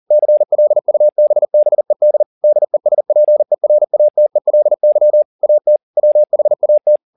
Morse code, heard as a keyed tone, about 40 wpm, 600 Hz. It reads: CLUBBED DESPERATELY AT WHAT
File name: the old man and the sea 2 40wpm